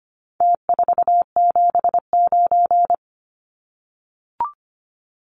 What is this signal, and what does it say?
Morse code, heard as a keyed tone, about 25 wpm, 700 Hz. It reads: T479
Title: Numbers - Contest Style - 4-Digit 25wpm